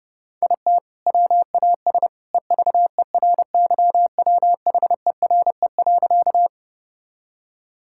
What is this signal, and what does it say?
Morse code, heard as a keyed tone, about 32 wpm, 700 Hz. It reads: IT WAS EVERYWHERE.
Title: Sentences from Top 1600 Words 30wpm